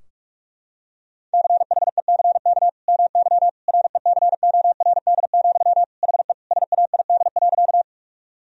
Morse code, heard as a keyed tone, about 45 wpm, 700 Hz. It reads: CHECK MY RECORD, HE SAID.
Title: Sentences from Top 1000 Words - Review Part 3 - 45wpm